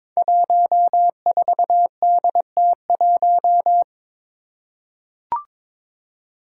Morse code, heard as a keyed tone, about 22 wpm, 700 Hz. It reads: 14DT1
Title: Five Challenging Letter-Number Combinations 22wpm